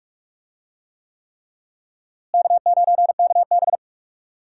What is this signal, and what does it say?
Morse code, heard as a keyed tone, about 46 wpm, 700 Hz. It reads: K9KB
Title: USA Call Signs - 4 Characters - Repeated 3x - 45wpm